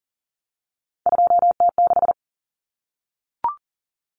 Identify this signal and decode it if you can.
Morse code, heard as a keyed tone, about 40 wpm, 700 Hz. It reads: 2T6
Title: Common Cut Numbers - 3-Digit 40wpm